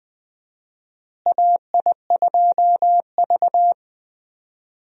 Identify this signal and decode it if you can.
Morse code, heard as a keyed tone, about 21 wpm, 700 Hz. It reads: AI2V
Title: USA Call Signs - 4 Characters - Repeated 3x - 20wpm